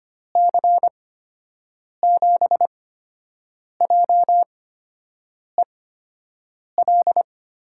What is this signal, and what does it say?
Morse code, heard as a keyed tone, about 25 wpm, 700 Hz. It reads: C 7 J E L
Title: USA Call Signs - 8x Character Spacing - 25wpm